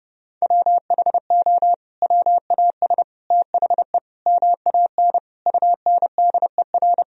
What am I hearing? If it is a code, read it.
Morse code, heard as a keyed tone, about 30 wpm, 700 Hz. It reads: WHO WAS THE MAN UNDER
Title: Sentences from Top 1000 Words - Review Part 1 - 30wpm